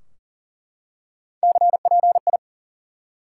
Morse code, heard as a keyed tone, about 40 wpm, 700 Hz. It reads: CPI
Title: Top 180 QSO Elements 40wpm